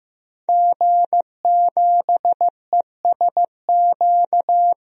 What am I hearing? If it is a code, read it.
Morse code, heard as a keyed tone, about 15 wpm, 700 Hz. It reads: G7ESQ
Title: DXCC-100 Call Signs - 5 Characters - Repeated 3x - 15wpm